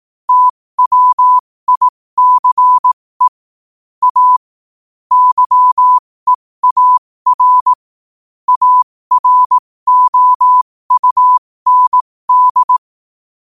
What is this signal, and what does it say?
Morse code, heard as a keyed tone, about 18 wpm, spaced as for 15 wpm, 1000 Hz. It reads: TWICE A YEAR AROUND
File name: qotdfiles / 15wpm